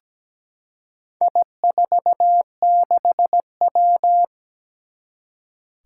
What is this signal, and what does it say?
Morse code, heard as a keyed tone, about 17 wpm, 700 Hz. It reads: I46W
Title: Four Challenging Letter-Number Combinations 17wpm